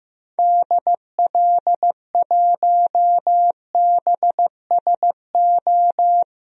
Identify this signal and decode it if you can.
Morse code, heard as a keyed tone, about 15 wpm, 700 Hz. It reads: DL1BSO
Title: DXCC-100 Call Signs - 6 Characters - Repeated 2x - 15wpm